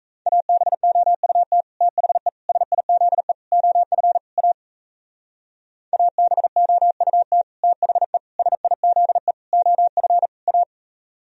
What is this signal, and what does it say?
Morse code, heard as a keyed tone, about 36 wpm, 700 Hz. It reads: ABOUT THE SIZE OF A ABOUT THE SIZE OF A